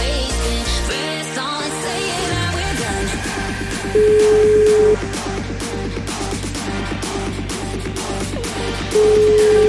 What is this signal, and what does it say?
Ringback tone (Europe, 425 Hz), with noise.